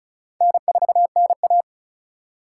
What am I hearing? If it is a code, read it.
Morse code, heard as a keyed tone, about 34 wpm, 700 Hz. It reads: N4NA
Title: USA Call Signs - 4 Characters - 4x Character Spacing - 35wpm